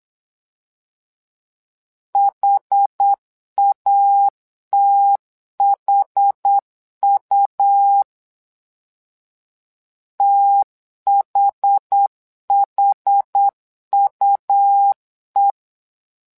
Morse code, heard as a keyed tone, about 8 wpm, 800 Hz. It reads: HATHU THHUE
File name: LEZIONI / 2 UHDN / 2 UH 8wpm 2129